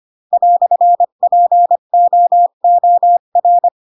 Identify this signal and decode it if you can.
Morse code, heard as a keyed tone, about 25 wpm, 700 Hz. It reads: "POOR
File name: Pinocchio / Pinocchio ch4 20 25 ik1hge